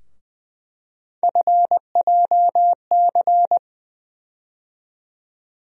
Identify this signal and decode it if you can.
Morse code, heard as a keyed tone, about 21 wpm, 700 Hz. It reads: FJC
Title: USA Call Sign Postfixes - 3 Letters 20wpm